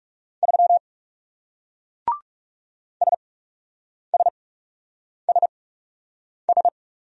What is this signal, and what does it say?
Morse code, heard as a keyed tone, about 42 wpm, spaced as for 8 wpm, 700 Hz. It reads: 3 SSSS